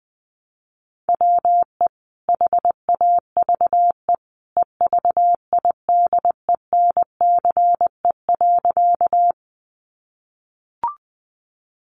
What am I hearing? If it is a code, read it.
Morse code, heard as a keyed tone, about 20 wpm, 700 Hz. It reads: WE HAVE EVIDENCE.
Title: Sentences from Top 1300 Words 20wpm